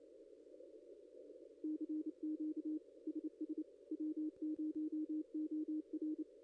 Morse code, heard as a keyed tone, about 28 wpm, 310 Hz. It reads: CQ SS W0OR
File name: contests / ss-cw-k5tr-op-n5rz -0900